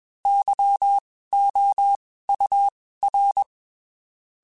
Morse code, heard as a keyed tone, about 21 wpm, spaced as for 15 wpm, 800 Hz. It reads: YOUR